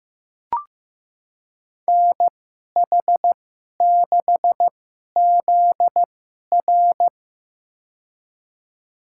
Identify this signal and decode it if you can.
Morse code, heard as a keyed tone, about 15 wpm, spaced as for 11 wpm, 700 Hz. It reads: NH6ZR
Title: USA Call Signs - 5 Characters - 2x Character Spacing - 15wpm